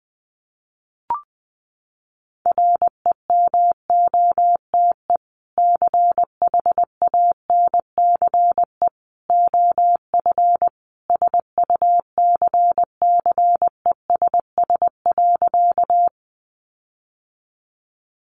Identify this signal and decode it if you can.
Morse code, heard as a keyed tone, about 20 wpm, 700 Hz. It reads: REMOTE CHANCE OF SUCCESS.
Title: Sentences from Top 1300 Words 20wpm